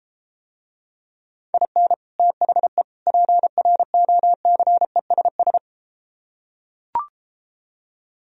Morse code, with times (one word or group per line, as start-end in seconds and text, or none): 1.5-1.9 IN
2.2-2.8 THE
3.1-5.6 PROCESS